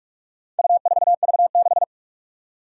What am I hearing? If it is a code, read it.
Morse code, heard as a keyed tone, about 46 wpm, 700 Hz. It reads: U4V6
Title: Four Challenging Letter-Number Combinations 45wpm